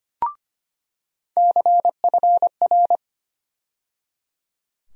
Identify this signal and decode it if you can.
Morse code, heard as a keyed tone, about 25 wpm, 700 Hz. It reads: CFR
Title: USA Call Sign Postfixes - 3 Letters 25wpm